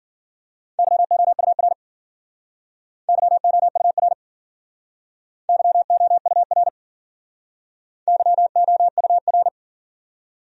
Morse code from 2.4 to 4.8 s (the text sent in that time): YOUR